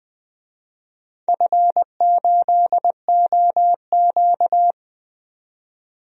Morse code, heard as a keyed tone, about 20 wpm, 700 Hz. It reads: F8OQ
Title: DXCC-100 Call Signs - 4 Characters - 20wpm